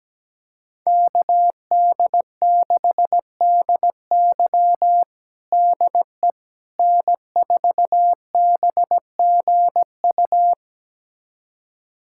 Morse code, heard as a keyed tone, about 17 wpm, 700 Hz. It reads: KD6DY DE N4BGU
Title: USA Callsign DE Callsign - 5 Characters - 17wpm